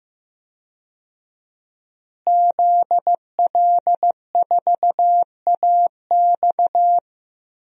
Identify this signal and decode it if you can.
Morse code, heard as a keyed tone, about 15 wpm, 700 Hz. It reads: ZL4AX
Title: DXCC-100 Call Signs - 5 Characters - Repeated 3x - 15wpm